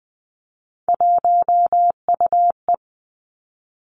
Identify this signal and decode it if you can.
Morse code, heard as a keyed tone, about 19 wpm, 700 Hz. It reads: 1UE
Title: Three Challenging Letter-Number Combinations 20wpm